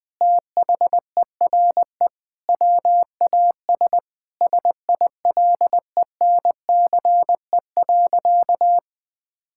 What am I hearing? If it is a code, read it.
Morse code, heard as a keyed tone, about 21 wpm, 700 Hz. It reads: THERE WAS SILENCE.
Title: Sentences with 5-Word Phrases - Spoken Beginning - 20wpm